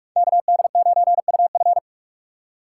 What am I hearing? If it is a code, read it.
Morse code, heard as a keyed tone, about 44 wpm, 700 Hz. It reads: KD9UF